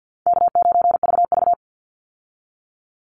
Morse code, heard as a keyed tone, about 50 wpm, 700 Hz. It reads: K9VV
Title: Call Signs - 50wpm